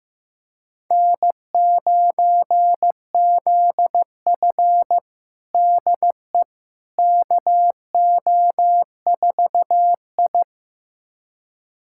Morse code, heard as a keyed tone, about 16 wpm, 700 Hz. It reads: N9ZF DE KO4I
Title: USA Callsign DE Callsign - 4 Characters - 15wpm